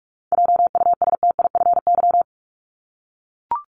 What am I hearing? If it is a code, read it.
Morse code, heard as a keyed tone, about 44 wpm, 700 Hz. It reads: JUSTIFY
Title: Top 1751-2000 Words 45wpm